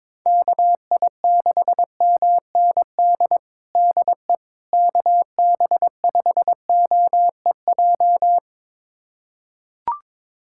Morse code, heard as a keyed tone, about 22 wpm, 700 Hz. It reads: KI6MND DE KB5OEJ
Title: USA Callsign DE Callsign 22wpm